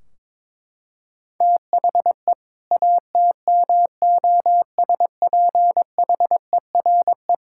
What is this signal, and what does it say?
Morse code, heard as a keyed tone, about 22 wpm, 700 Hz. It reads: THE ATMOSPHERE
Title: Sentences from Top 1600 Words 22wpm